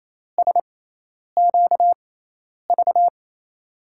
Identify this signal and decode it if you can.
Morse code, heard as a keyed tone, about 28 wpm, spaced as for 10 wpm, 700 Hz. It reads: SQV
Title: USA Call Signs - 5 Characters - 6x Character Spacing - 28wpm